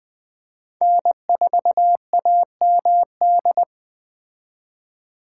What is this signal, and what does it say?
Morse code, heard as a keyed tone, about 20 wpm, 700 Hz. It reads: N4AMD